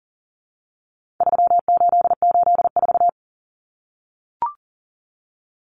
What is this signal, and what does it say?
Morse code, heard as a keyed tone, about 40 wpm, 700 Hz. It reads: 3884
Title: Common Cut Numbers - 4-Digit 40wpm